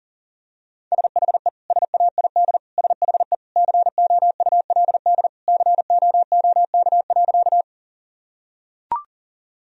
Morse code, heard as a keyed tone, about 40 wpm, 700 Hz. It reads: SHE SAID SHE COULD COOK.
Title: Sentences from Top 1000 Words - Review Part 4 - 40wpm